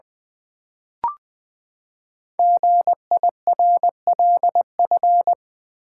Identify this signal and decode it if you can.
Morse code, heard as a keyed tone, about 20 wpm, 700 Hz. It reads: GIRLF
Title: Build A Word - 5 Letters At A Time - 10-Letter Words - 20wpm